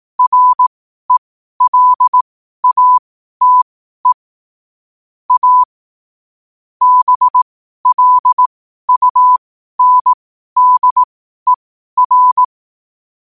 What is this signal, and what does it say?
Morse code, heard as a keyed tone, about 18 wpm, spaced as for 12 wpm, 1000 Hz. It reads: RELATE A BLUNDER
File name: 12wpm